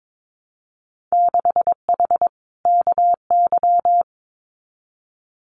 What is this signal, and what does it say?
Morse code, heard as a keyed tone, about 22 wpm, 700 Hz. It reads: 6H KY
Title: Winter Field Day Exchanges Encore 22wpm